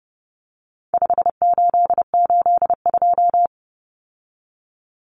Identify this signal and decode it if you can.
Morse code, heard as a keyed tone, about 29 wpm, 700 Hz. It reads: 5882
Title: Numbers - 4-Digit 30wpm